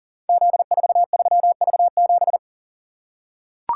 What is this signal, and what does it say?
Morse code, heard as a keyed tone, about 38 wpm, 690 Hz. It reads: Z43V7
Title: Code Groups - 5 Letter:Numbers - 40wpm